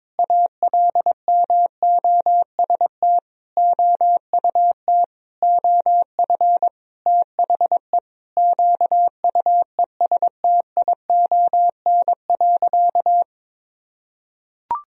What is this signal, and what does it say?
Morse code, heard as a keyed tone, about 21 wpm, 700 Hz. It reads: ALMOST OUT OF THE QUESTION.
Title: Sentences from Top 1000 Words - Review Part 4 - 22wpm